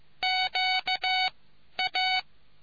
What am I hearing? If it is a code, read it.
Morse code, heard as a keyed tone, about 15 wpm, 750 Hz. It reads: Q A